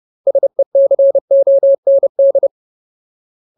Morse code, heard as a keyed tone, about 30 wpm, 550 Hz. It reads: SECOND
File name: Ultimate Word List / 30WPM Ultimate